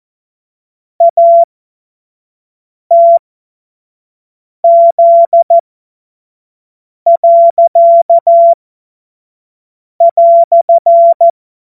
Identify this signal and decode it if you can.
Morse code, heard as a keyed tone, about 14 wpm, spaced as for 5 wpm, 680 Hz. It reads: ATZ."
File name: Morsekurs / Witze / Witz 07. März 2023 25 F70